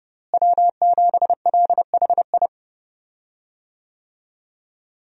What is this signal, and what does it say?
Morse code, heard as a keyed tone, about 30 wpm, 700 Hz. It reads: W7LHI
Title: USA Call Signs - 5 Characters - Repeated 3x - 30wpm